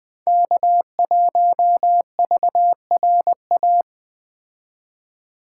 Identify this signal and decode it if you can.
Morse code, heard as a keyed tone, about 20 wpm, 700 Hz. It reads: K1VRA